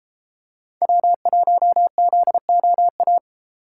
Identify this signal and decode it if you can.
Morse code, heard as a keyed tone, about 32 wpm, 700 Hz. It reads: W1ZOA